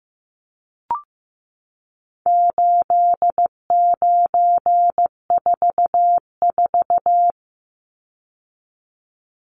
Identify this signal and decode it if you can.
Morse code, heard as a keyed tone, about 15 wpm, 700 Hz. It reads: 8944
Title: Numbers - 4-Digit 15wpm